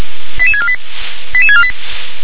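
Five-tone selective call (ZVEI), heard as quick runs of five tones, tones of about 70 ms. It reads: 88648 88648
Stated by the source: Received on an SDR; ZVEI five-tone selective call.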